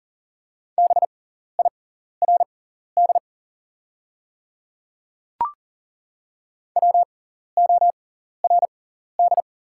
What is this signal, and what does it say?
Morse code, heard as a keyed tone, about 40 wpm, spaced as for 14 wpm, 700 Hz. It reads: BIRD WORD
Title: 6x Character Spacing - Top 500 Words - 40wpm